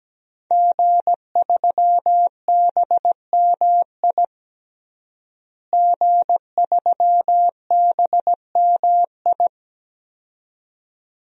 Morse code, heard as a keyed tone, about 17 wpm, 700 Hz. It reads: G3BMI G3BMI
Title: DXCC-100 Call Signs - 5 Characters - Repeated 2x - 17wpm